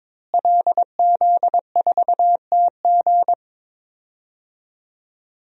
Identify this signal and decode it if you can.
Morse code, heard as a keyed tone, about 22 wpm, 700 Hz. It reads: LZ4TG
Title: DXCC-100 Call Signs - 5 Characters - Repeated 2x - 22wpm